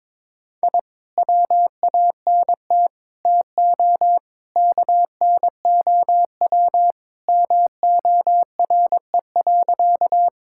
Morse code, heard as a keyed tone, about 22 wpm, 700 Hz. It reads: I WANT TO KNOW MORE.
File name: practice / 22wpm / Sentences with 2-Word Phrases - 22wpm